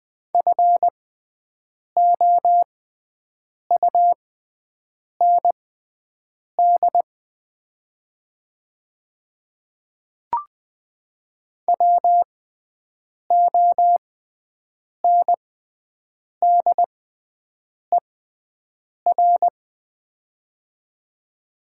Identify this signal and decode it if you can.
Morse code, heard as a keyed tone, about 20 wpm, spaced as for 7 wpm, 700 Hz. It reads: FOUND WONDER
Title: 6x Character Spacing - Top 500 Words - 20wpm